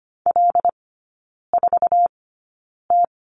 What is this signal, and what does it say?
Morse code, heard as a keyed tone, about 25 wpm, spaced as for 9 wpm, 700 Hz. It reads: L4T
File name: Lesson 17e - Letter u - Three Characters 25wpm